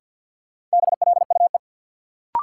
Morse code, heard as a keyed tone, about 53 wpm, 700 Hz. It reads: BLUE
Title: Sentences from Top 1200 Words - New Word Drill Alt - 50wpm